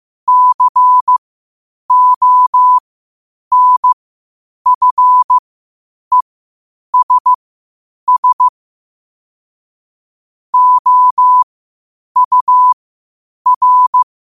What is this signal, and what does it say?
Morse code, heard as a keyed tone, about 15 wpm, spaced as for 8 wpm, 1000 Hz. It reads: CONFESS OUR
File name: qotdfiles / 8wpm